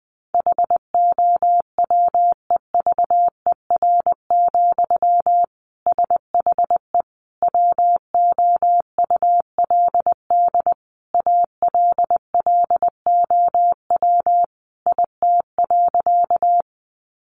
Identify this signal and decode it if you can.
Morse code, heard as a keyed tone, about 20 wpm, 700 Hz. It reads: HOWEVER, SHE WOULD ALLOW IT.
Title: Sentences from Top 700 Words 20wpm